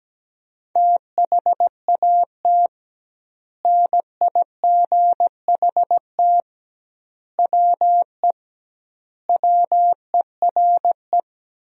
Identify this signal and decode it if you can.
Morse code, heard as a keyed tone, about 17 wpm, 700 Hz. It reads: THAT NIGHT WE WERE
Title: 2x Word-Spacing - Sentences from Top 300 Words - 17wpm